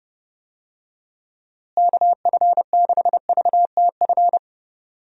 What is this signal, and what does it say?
Morse code, heard as a keyed tone, about 32 wpm, 700 Hz. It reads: KF6VTF
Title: USA Call Signs - 6 Characters - Repeated 3x - 30wpm